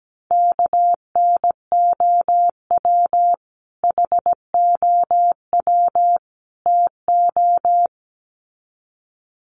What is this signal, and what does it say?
Morse code, heard as a keyed tone, about 17 wpm, 700 Hz. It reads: KNOW HOW TO
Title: Sets of 3 Words 17wpm